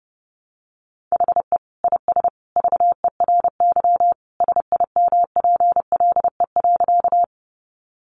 Morse code, heard as a keyed tone, about 28 wpm, 700 Hz. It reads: HE IS VERY SIMPLE.
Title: Sentences from Top 400 Words 30wpm